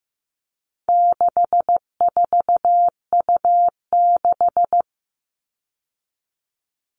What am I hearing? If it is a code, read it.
Morse code, heard as a keyed tone, about 15 wpm, 700 Hz. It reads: 64U6
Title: All Cut Numbers - 4-Digit 15wpm